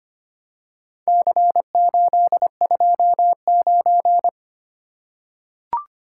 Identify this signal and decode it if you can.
Morse code, heard as a keyed tone, about 25 wpm, 700 Hz. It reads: C829
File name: practice / 25wpm / Four Letter-Number Combinations - 25wpm